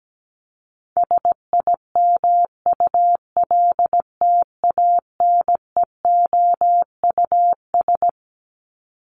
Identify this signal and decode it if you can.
Morse code, heard as a keyed tone, about 17 wpm, 700 Hz. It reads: SIMULTANEOUS
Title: LY Suffix - Top 100 Words with Base Words 17wpm